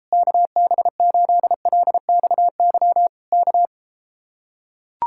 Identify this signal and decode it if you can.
Morse code, heard as a keyed tone, about 33 wpm, 700 Hz. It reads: KB8LXY K